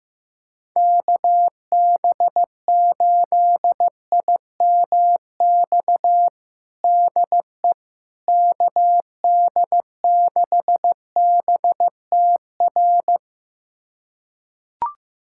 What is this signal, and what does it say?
Morse code, heard as a keyed tone, about 16 wpm, 700 Hz. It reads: KB8IMX DE KD6BTR
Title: USA Callsign DE Callsign 15wpm